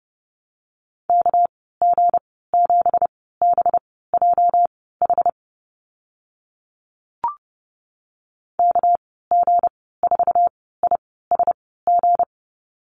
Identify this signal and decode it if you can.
Morse code, heard as a keyed tone, about 30 wpm, spaced as for 17 wpm, 700 Hz. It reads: KG7BJH KG4ISG
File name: Lesson 37g - Letter x - Call Signs 30wpm